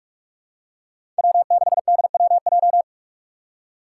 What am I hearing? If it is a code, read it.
Morse code, heard as a keyed tone, about 46 wpm, 700 Hz. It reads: W6DWJ
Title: Five Challenging Letter-Number Combinations 45wpm